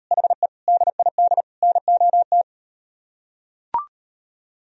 Morse code, heard as a keyed tone, about 38 wpm, 700 Hz. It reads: HE DID NOT